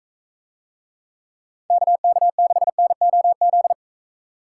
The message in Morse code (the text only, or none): KK6NOZ